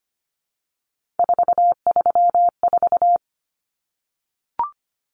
Morse code, heard as a keyed tone, about 25 wpm, 700 Hz. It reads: 434
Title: Common Cut Numbers - 3-Digit 25wpm